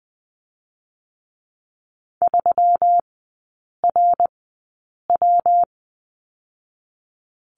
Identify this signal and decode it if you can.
Morse code, heard as a keyed tone, about 20 wpm, 700 Hz. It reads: 3 R W
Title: 121 Lesson 20e - Number 3 - Three Characters 20wpm